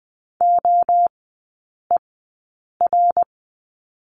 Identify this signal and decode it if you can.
Morse code, heard as a keyed tone, about 20 wpm, spaced as for 8 wpm, 700 Hz. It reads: OER
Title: 053 Lesson 10e - Letter r - Three Characters 20wpm